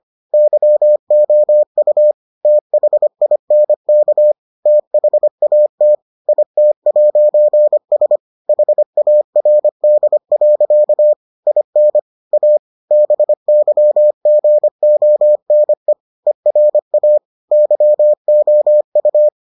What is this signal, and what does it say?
Morse code, heard as a keyed tone, about 25 wpm, 600 Hz. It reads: YOU THINK THAT IT'S HARD. IN A BYGONE ERA YOU